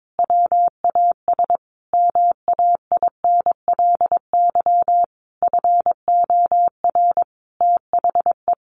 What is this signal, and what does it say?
Morse code, heard as a keyed tone, about 22 wpm, 700 Hz. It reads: WAS MAINLY FOR THE
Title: Sentences from Top 1400 Words 22wpm